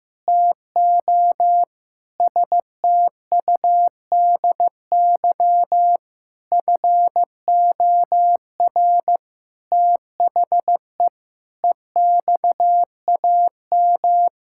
Morse code, heard as a keyed tone, about 15 wpm, 700 Hz. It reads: TO STUDY FOR THE EXAM
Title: Sentences from Top 1000 Words - Review Part 5 - 15wpm